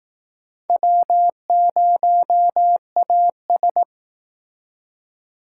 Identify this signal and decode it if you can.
Morse code, heard as a keyed tone, about 19 wpm, 700 Hz. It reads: W0AS